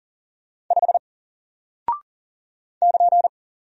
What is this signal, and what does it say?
Morse code, heard as a keyed tone, about 40 wpm, 700 Hz. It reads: 5 <KN>
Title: Single Character (Letter:Number:Punctuation:Prosign) 40wpm